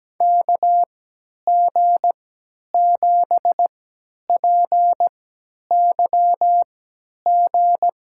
Morse code, heard as a keyed tone, about 17 wpm, spaced as for 10 wpm, 700 Hz. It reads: KG7PYG
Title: USA Call Signs - 3x Character Spacing - 17wpm